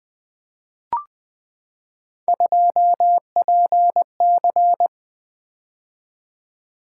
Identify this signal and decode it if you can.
Morse code, heard as a keyed tone, about 20 wpm, 700 Hz. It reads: 2PC
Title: Three Letter-Number Combinations - 20wpm